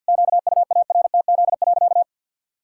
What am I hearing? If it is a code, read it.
Morse code, heard as a keyed tone, about 50 wpm, 700 Hz. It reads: QUARTZ.
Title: Pangram Challenge v2 - 50wpm